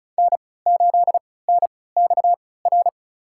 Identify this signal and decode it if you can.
Morse code, heard as a keyed tone, about 34 wpm, spaced as for 20 wpm, 700 Hz. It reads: N8NXR
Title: USA Call Signs - 5 Characters - 3x Character Spacing - 35wpm